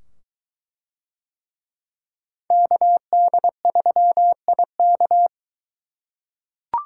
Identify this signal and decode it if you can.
Morse code, heard as a keyed tone, about 21 wpm, 700 Hz. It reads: KD3IK